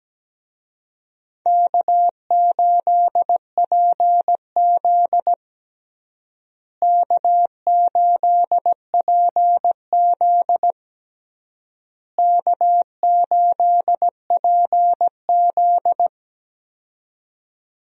Morse code, heard as a keyed tone, about 17 wpm, 700 Hz. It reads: K8PZ K8PZ K8PZ
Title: USA Call Signs - 4 Characters - Repeated 3x - 17wpm